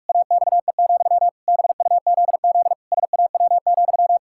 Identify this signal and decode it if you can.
Morse code, heard as a keyed tone, about 42 wpm, 700 Hz. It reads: AXE, BUZZ SAW,
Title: Pangram Challenge v2 - 45wpm